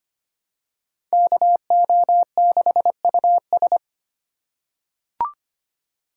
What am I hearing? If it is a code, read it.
Morse code, heard as a keyed tone, about 25 wpm, 700 Hz. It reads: KO6US
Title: USA Call Signs - 5 Characters - 3x Character Spacing - 25wpm